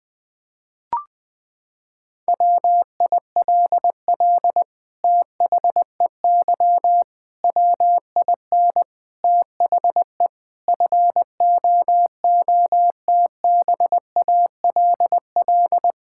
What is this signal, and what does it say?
Morse code, heard as a keyed tone, about 19 wpm, 700 Hz. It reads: WILL THEY WIN THE FOOTBALL
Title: Sentences from Top 1600 Words 20wpm